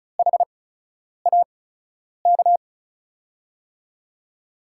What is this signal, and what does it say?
Morse code, heard as a keyed tone, about 34 wpm, spaced as for 10 wpm, 700 Hz. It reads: HAK
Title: USA Call Signs - 6 Characters - 8x Character Spacing - 35wpm